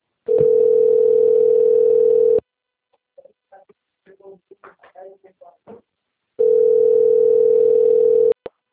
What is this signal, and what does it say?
Ringback tone (North America), 440 + 480 Hz.